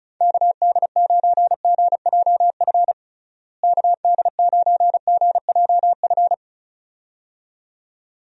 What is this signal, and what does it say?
Morse code, heard as a keyed tone, about 36 wpm, 700 Hz. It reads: KD9GJF KD9GJF
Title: USA Call Signs - 6 Characters - Repeated 2x - 35wpm